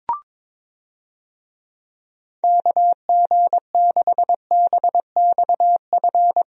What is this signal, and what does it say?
Morse code, heard as a keyed tone, about 22 wpm, 700 Hz. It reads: KG6BXF
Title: USA Call Signs - 6 Characters - Repeated 3x - 22wpm